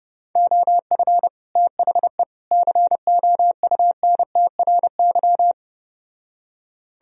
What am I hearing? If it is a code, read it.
Morse code, heard as a keyed tone, about 33 wpm, 700 Hz. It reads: OF THE COUNTRY